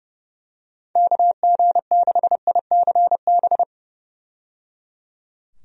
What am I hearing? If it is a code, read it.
Morse code, heard as a keyed tone, about 30 wpm, 700 Hz. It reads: KG6ICB
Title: USA Call Signs - 6 Characters - 30wpm